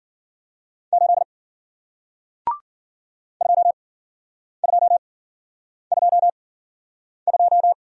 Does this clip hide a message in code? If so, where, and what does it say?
Yes: Morse code from 0.9 to 7.7 s, heard as a keyed tone, about 52 wpm, spaced as for 9 wpm, 700 Hz. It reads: 8 2222